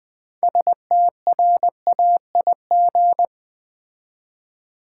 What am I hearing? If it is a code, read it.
Morse code, heard as a keyed tone, about 20 wpm, 700 Hz. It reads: STRAIG
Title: Build A Word - 2 Letters At A Time - 8-Letter Words - 20wpm